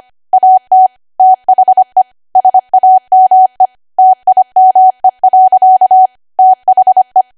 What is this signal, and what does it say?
Morse code, heard as a keyed tone, about 25 wpm, 750 Hz. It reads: AT THE SAME TIME. THE
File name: cwa / W1AW-Level3 / 25WPM